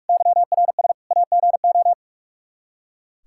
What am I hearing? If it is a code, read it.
Morse code, heard as a keyed tone, about 41 wpm, 700 Hz. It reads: YRS AGO
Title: Practice QSO - Exchange 3 - More Info - 45wpm